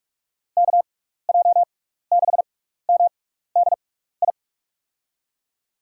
Morse code, heard as a keyed tone, about 42 wpm, spaced as for 15 wpm, 700 Hz. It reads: KJ6MDI
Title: USA Call Signs - 6 Characters - 6x Character Spacing - 45wpm